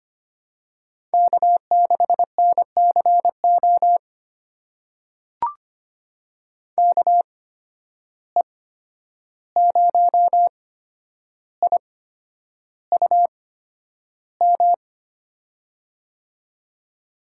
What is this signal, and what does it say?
Morse code, heard as a keyed tone, about 25 wpm, spaced as for 10 wpm, 700 Hz. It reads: K6NCO KE0IUM
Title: USA Call Signs - 8x Character Spacing - 25wpm